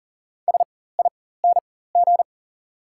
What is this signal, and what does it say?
Morse code, heard as a keyed tone, about 40 wpm, spaced as for 19 wpm, 690 Hz. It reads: SING
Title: 4x Character Spacing - Top 500 Words - 40wpm